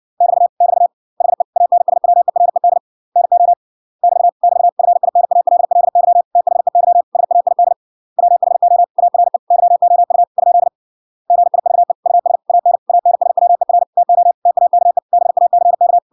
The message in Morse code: <BT> <BT> HE ANSWERED NO <BT> <BT> FINALLY THEY SAID WHO ARE YOU ? GIVE US AN ANSWER TO TAKE BACK

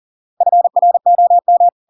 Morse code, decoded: FROM

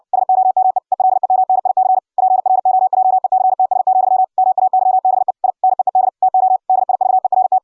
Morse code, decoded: SOME ELEMNIC PROJECTS, MAYBE I NEED TO DIVERT

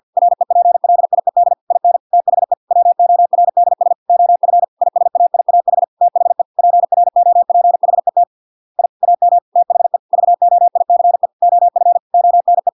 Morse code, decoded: REPLIED IN THE WORDS OF ISAIAH THE PROPHET I AM THE VOICE OF ONE

